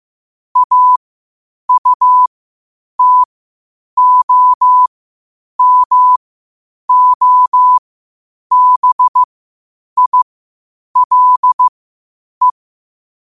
Morse code, heard as a keyed tone, about 15 wpm, spaced as for 8 wpm, 1000 Hz. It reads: AUTOMOBILE
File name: qotdfiles / 8wpm